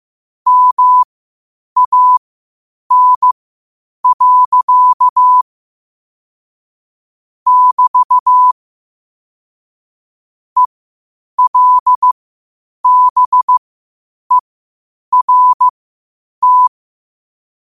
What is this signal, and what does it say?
Morse code, heard as a keyed tone, about 15 wpm, spaced as for 8 wpm, 1000 Hz. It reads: MAN. <BT> ELBERT
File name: qotdfiles / 8wpm